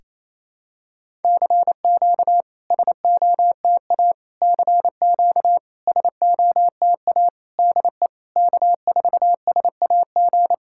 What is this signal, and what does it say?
Morse code, heard as a keyed tone, about 29 wpm, 690 Hz. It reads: CQ SOTA CQ SOTA DE K4SAG
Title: SOTA QSO Practice 28wpm